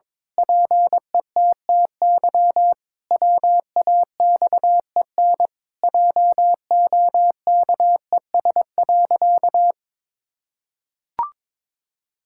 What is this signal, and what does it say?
Morse code, heard as a keyed tone, about 22 wpm, 700 Hz. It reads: PETTY WAXEN JOKES.
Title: Pangram Challenge v2 - 22wpm